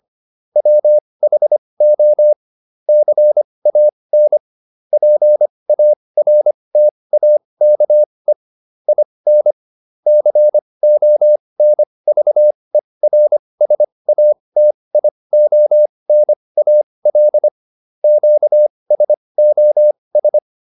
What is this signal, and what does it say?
Morse code, heard as a keyed tone, about 25 wpm, spaced as for 20 wpm, 600 Hz. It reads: WHO CAN PARTAKE IN CONVERSATIONAL QSOS